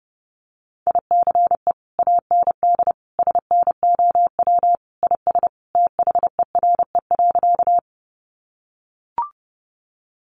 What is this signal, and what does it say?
Morse code, heard as a keyed tone, about 30 wpm, 700 Hz. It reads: ICE AND SNOW IS THERE.
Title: Sentences from Top 1000 Words 30wpm